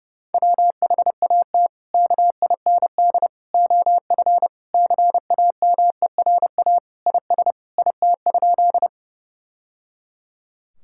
Morse code, heard as a keyed tone, about 32 wpm, 700 Hz. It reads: WHAT KIND OF CAMERA IS IT?
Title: Sentences from Top 1000 Words - Review Part 5 - 30wpm